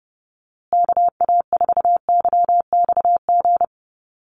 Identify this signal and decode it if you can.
Morse code, heard as a keyed tone, about 30 wpm, 700 Hz. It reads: KA4YXG
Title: USA Call Signs Repeated 2x 30wpm